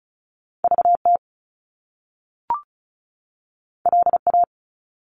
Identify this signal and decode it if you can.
Morse code, heard as a keyed tone, about 36 wpm, 700 Hz. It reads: VT LA
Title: US States 35wpm